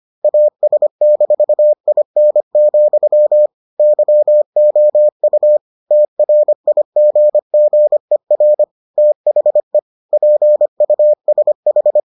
Morse code, heard as a keyed tone, about 25 wpm, 600 Hz. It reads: AS-IN, YOU TRIGGER THE PUSH